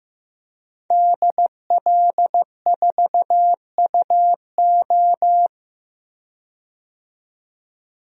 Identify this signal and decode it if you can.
Morse code, heard as a keyed tone, about 15 wpm, 700 Hz. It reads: DL4UO
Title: DXCC-100 Call Signs - 5 Characters - 15wpm